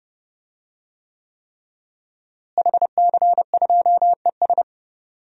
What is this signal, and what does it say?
Morse code, heard as a keyed tone, about 32 wpm, 700 Hz. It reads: HC2ES
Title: DXCC-100 Call Signs - 5 Characters - Repeated 3x - 30wpm